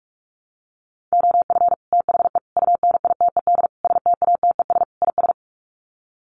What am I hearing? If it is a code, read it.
Morse code, heard as a keyed tone, about 44 wpm, 700 Hz. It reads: OF THE UNITED STATES IS
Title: Sets of 5 Words Encore2 45wpm